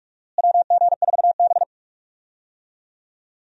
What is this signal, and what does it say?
Morse code, heard as a keyed tone, about 42 wpm, 700 Hz. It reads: WG4B
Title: USA Call Signs - 4 Characters - Repeated 2x - 45wpm